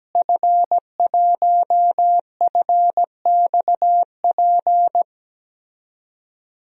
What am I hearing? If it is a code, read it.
Morse code, heard as a keyed tone, about 17 wpm, 700 Hz. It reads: F1FXP
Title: DXCC-100 Call Signs - 5 Characters - Repeated 3x - 17wpm